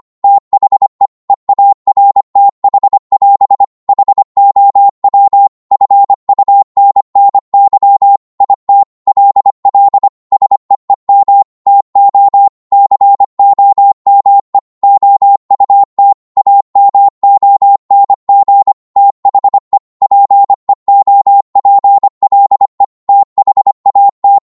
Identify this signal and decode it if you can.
Morse code, heard as a keyed tone, about 25 wpm, 800 Hz. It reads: THE EARTH<AS> HOW FUNNY IT LL SEEM TO COME OUT AMONG THE PEOPLE THAT